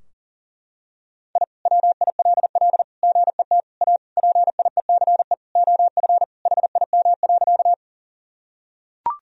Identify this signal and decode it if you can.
Morse code, heard as a keyed tone, about 37 wpm, 700 Hz. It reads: I WILL GET A PIECE OF HIM.
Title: Sentences from Top 1000 Words - Review Part 2 - 40wpm